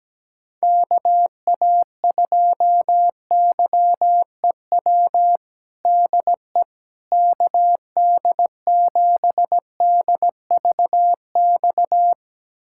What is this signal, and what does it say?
Morse code, heard as a keyed tone, about 17 wpm, 700 Hz. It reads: KA2YEW DE KD7DVX
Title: USA Callsign DE Callsign - 6 Characters - 17wpm